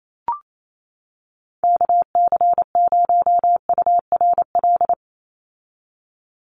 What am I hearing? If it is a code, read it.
Morse code, heard as a keyed tone, about 28 wpm, 700 Hz. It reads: KC0URL
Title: Call Signs - 28wpm